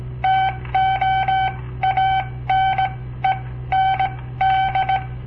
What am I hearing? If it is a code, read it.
Morse code, heard as a keyed tone, about 17 wpm, spaced as for 14 wpm, 770 Hz. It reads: TO ANE ND
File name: papertapes / local / Code Practice-01